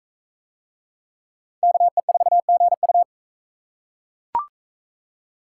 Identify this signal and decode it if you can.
Morse code, heard as a keyed tone, about 41 wpm, 700 Hz. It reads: KE4GU